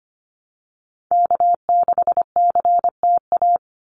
Morse code, heard as a keyed tone, about 25 wpm, 700 Hz. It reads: K6CTA
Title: Call Signs Repeated 2x - 25wpm